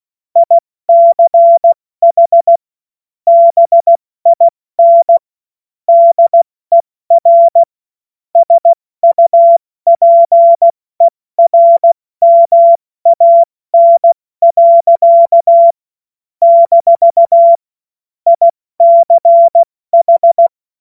Morse code, heard as a keyed tone, about 16 wpm, 680 Hz. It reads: ICH BIN DER SUPERMAN. - ICH